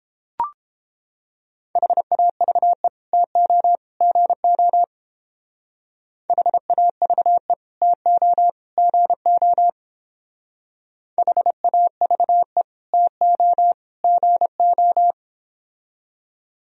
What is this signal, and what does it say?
Morse code, heard as a keyed tone, about 28 wpm, 700 Hz. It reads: HAVE TO GO HAVE TO GO HAVE TO GO